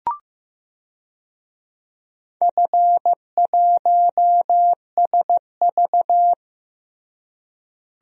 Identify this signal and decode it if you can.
Morse code, heard as a keyed tone, about 16 wpm, 700 Hz. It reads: F1SV
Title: DXCC-100 Call Signs - 4 Characters - Repeated 3x - 15wpm